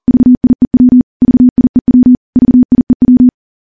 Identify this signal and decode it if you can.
Morse code, heard as a keyed tone, about 39 wpm, 250 Hz. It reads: VIEW VIEW VIEW
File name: morse / morse-training-40WPM